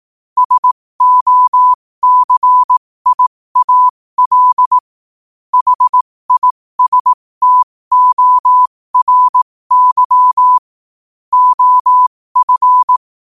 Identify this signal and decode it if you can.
Morse code, heard as a keyed tone, about 18 wpm, spaced as for 15 wpm, 1000 Hz. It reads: SOCIAL HISTORY OF
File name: qotdfiles / 15wpm